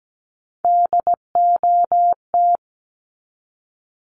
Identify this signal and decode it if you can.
Morse code, heard as a keyed tone, about 17 wpm, 700 Hz. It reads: DOT
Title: Top 1251-1500 Words 17wpm